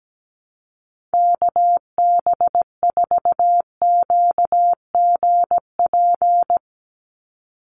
Morse code, heard as a keyed tone, about 17 wpm, 700 Hz. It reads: KB4QGP
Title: USA Call Signs 17wpm